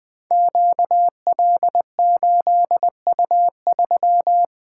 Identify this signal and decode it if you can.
Morse code, heard as a keyed tone, about 20 wpm, 700 Hz. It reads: QL8U3
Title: Code Groups - 5 Letter:Numbers - 20wpm